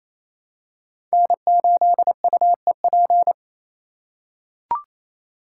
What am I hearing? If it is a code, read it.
Morse code, heard as a keyed tone, about 27 wpm, 700 Hz. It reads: N8UEP
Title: Code Groups - 5 Letter:Numbers - 28wpm